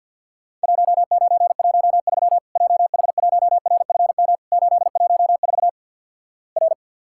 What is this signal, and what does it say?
Morse code, heard as a keyed tone, about 50 wpm, 700 Hz. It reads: 1913 JH1RFM 814 R
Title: Worked All Europe DX Contest - QTC Exchange with Repeats - 50wpm